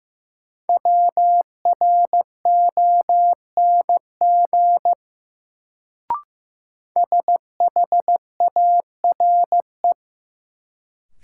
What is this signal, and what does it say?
Morse code, heard as a keyed tone, about 15 wpm, 700 Hz. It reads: WRONG SHARE
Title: Sentences from Top 900 Words - New Word Drill - 15wpm